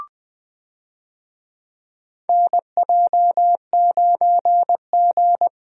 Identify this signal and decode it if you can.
Morse code, heard as a keyed tone, about 21 wpm, 700 Hz. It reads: NJ9G
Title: USA Call Signs - 4 Characters - Repeated 2x - 20wpm